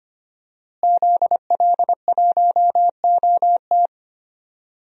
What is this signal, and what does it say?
Morse code, heard as a keyed tone, about 25 wpm, 700 Hz. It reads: ZL1OT
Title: DXCC-100 Call Signs - 5 Characters - Repeated 2x - 25wpm